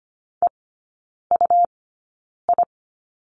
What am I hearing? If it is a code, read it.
Morse code, heard as a keyed tone, about 25 wpm, spaced as for 9 wpm, 700 Hz. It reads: EUI
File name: Lesson 17e - Letter u - Three Characters 25wpm